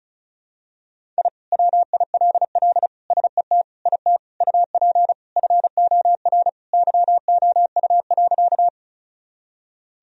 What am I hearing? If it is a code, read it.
Morse code, heard as a keyed tone, about 36 wpm, 700 Hz. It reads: I WILL SET IT UP FOR YOU.
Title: Sentences from Top 1000 Words - Review Part 1 - 35wpm